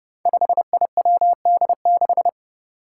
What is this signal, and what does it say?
Morse code, heard as a keyed tone, about 30 wpm, 700 Hz. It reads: 5IWD6
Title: Five Challenging Letter-Number Combinations 30wpm